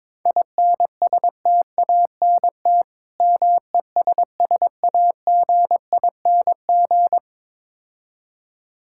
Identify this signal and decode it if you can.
Morse code, heard as a keyed tone, about 23 wpm, 690 Hz. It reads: INSTANT MESSAGING
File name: Easy Words - Speed-Racing with Hints - 22wpm